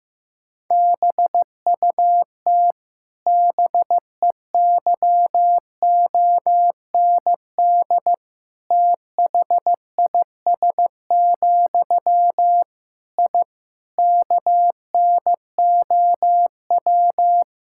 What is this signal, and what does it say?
Morse code, heard as a keyed tone, about 15 wpm, 700 Hz. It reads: BUT BEYOND THIS, I KNOW
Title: Sentences from Top 1000 Words - Review Part 6 - 15wpm